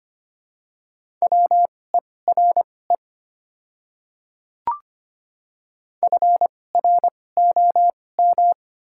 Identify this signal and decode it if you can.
Morse code, heard as a keyed tone, about 25 wpm, spaced as for 18 wpm, 700 Hz. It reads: WERE FROM
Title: 2x Character Spacing - Top 300 Words - 25wpm